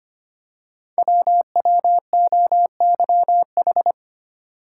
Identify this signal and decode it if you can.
Morse code, heard as a keyed tone, about 25 wpm, 700 Hz. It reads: WWOYH
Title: Code Groups - 5 Letters - 25wpm